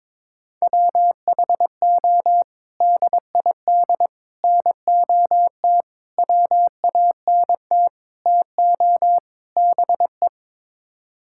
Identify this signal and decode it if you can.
Morse code, heard as a keyed tone, about 22 wpm, 700 Hz. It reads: WHO DID NOT WANT TO BE
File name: practice / 22wpm / Sets of 6 Words - 22wpm